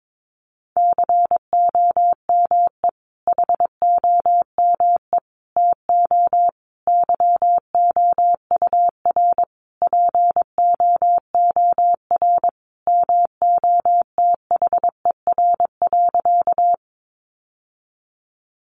Morse code, heard as a keyed tone, about 22 wpm, 700 Hz. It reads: COME HOME TO YOUR POOR MOTHER.
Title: Sentences from Top 1000 Words 22wpm